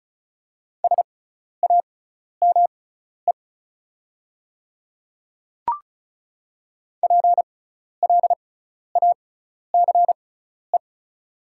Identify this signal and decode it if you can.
Morse code, heard as a keyed tone, about 36 wpm, spaced as for 12 wpm, 690 Hz. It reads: SAME PLACE
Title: 6x Character Spacing - Top 300 Words - 35wpm